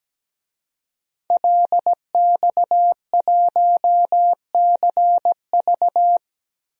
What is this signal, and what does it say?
Morse code, heard as a keyed tone, about 17 wpm, 700 Hz. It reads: LX1CV
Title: DXCC-100 Call Signs - 5 Characters - 17wpm